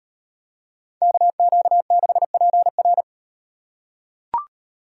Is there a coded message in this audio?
Yes: Morse code from 1.0 to 3.0 s, heard as a keyed tone, about 38 wpm, 700 Hz. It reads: KQ6PR